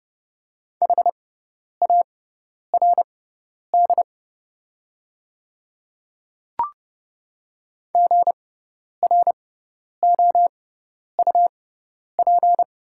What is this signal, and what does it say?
Morse code, heard as a keyed tone, about 31 wpm, spaced as for 10 wpm, 700 Hz. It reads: HARD GROUP
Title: 6x Character Spacing - Top 300 Words - 30wpm